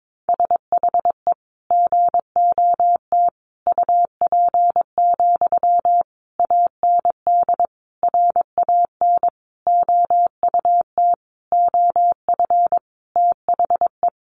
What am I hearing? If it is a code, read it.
Morse code, heard as a keyed tone, about 22 wpm, 700 Hz. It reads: SHE GOT UP, AND RAN OUT OF THE
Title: Sentences from Top 500 Words 22wpm